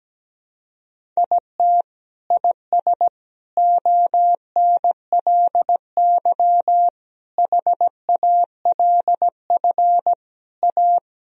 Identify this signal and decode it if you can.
Morse code, heard as a keyed tone, about 17 wpm, 700 Hz. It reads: IT IS ONLY HALF A
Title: Sentences from Top 1000 Words - Review Part 2 - 17wpm